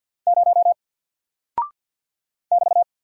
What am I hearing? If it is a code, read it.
Morse code, heard as a keyed tone, about 49 wpm, 700 Hz. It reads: 0 <BT>
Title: Single Character (Letter:Number:Punctuation:Prosign) 50wpm